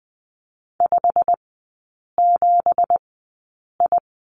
Morse code, heard as a keyed tone, about 20 wpm, spaced as for 8 wpm, 700 Hz. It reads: 57I
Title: 172 Lesson 28e - Number 9 - Three Characters 20wpm